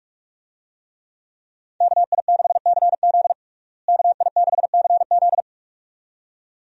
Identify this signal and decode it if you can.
Morse code, heard as a keyed tone, about 46 wpm, 700 Hz. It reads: KI6CZ KI6CZ
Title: USA Call Signs - 5 Characters - Repeated 2x - 45wpm